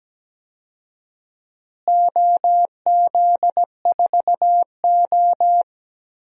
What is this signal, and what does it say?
Morse code, heard as a keyed tone, about 17 wpm, 700 Hz. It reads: OZ4O
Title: DXCC-100 Call Signs - 4 Characters - Repeated 2x - 17wpm